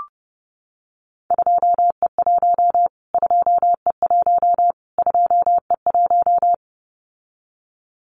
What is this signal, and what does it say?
Morse code, heard as a keyed tone, about 30 wpm, 700 Hz. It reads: 2E1 2E1 2E1
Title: Callsign Prefixes for Top 25 DXCC - Repeated 3x - 30wpm